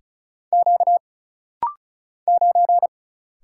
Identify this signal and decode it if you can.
Morse code, heard as a keyed tone, about 35 wpm, 700 Hz. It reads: Q 9
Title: Single Letter-Number Alt - Rapid-Fire 35wpm